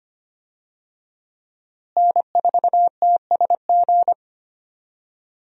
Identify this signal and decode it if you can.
Morse code, heard as a keyed tone, about 25 wpm, 700 Hz. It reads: N4TSG
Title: USA Call Signs - 5 Characters - Repeated 3x - 25wpm